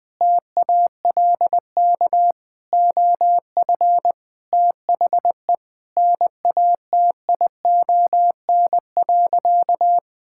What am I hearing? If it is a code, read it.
Morse code, heard as a keyed tone, about 20 wpm, 700 Hz. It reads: TALK OF THE NATION.
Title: Sentences with 5-Word Phrases - Spoken Beginning - 20wpm